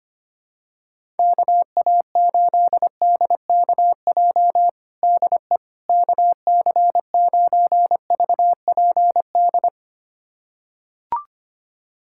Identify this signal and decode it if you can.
Morse code, heard as a keyed tone, about 25 wpm, 700 Hz. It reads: KA8DKJ DE KC9VPD
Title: USA Callsign DE Callsign - 6 Characters - 25wpm